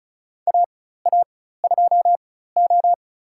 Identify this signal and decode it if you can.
Morse code, heard as a keyed tone, about 36 wpm, spaced as for 16 wpm, 700 Hz. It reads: AA2O
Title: USA Call Signs - 4 Characters - 4x Character Spacing - 35wpm